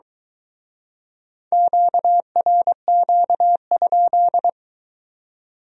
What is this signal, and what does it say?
Morse code, heard as a keyed tone, about 23 wpm, 700 Hz. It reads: QRQ?